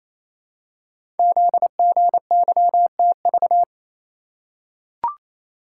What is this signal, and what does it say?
Morse code, heard as a keyed tone, about 27 wpm, 700 Hz. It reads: ZGYTV
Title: Code Groups - 5 Letters - 28wpm